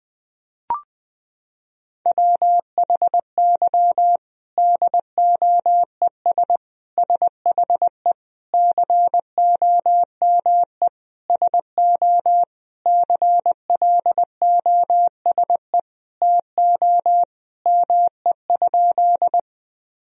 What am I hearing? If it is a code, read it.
Morse code, heard as a keyed tone, about 21 wpm, 700 Hz. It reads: WHY DOES SHE COME SO CLOSE TO ME?
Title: Sentences from Top 1000 Words - Review Part 1 - 20wpm